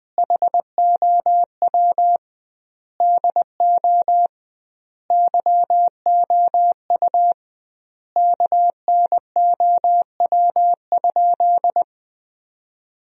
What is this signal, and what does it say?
Morse code, heard as a keyed tone, about 21 wpm, 700 Hz. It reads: HOW DO YOU KNOW?
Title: 2x Word-Spacing - Sentences from Top 100 Words - 20wpm